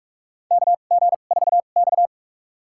Morse code, heard as a keyed tone, about 46 wpm, spaced as for 33 wpm, 700 Hz. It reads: KG4X
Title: USA Call Signs - 4 Characters - 2x Character Spacing - 45wpm